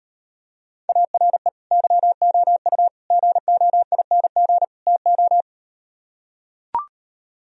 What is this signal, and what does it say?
Morse code, heard as a keyed tone, about 38 wpm, 700 Hz. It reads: ARE YOU GOING TO